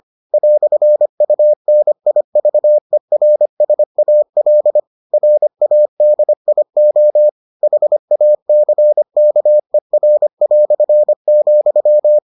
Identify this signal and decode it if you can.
Morse code, heard as a keyed tone, about 25 wpm, 600 Hz. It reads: "UNIVERSAL RADIO HACKER",